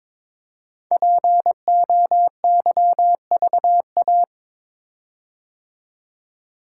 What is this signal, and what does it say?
Morse code, heard as a keyed tone, about 22 wpm, 700 Hz. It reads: POYVA
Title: Code Groups - 5 Letters - 22wpm